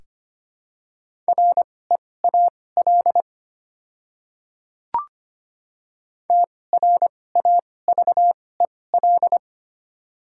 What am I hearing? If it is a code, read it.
Morse code, heard as a keyed tone, about 25 wpm, spaced as for 18 wpm, 700 Hz. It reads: REAL TRAVEL
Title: 2x Character Spacing - Top 500 Words - 25wpm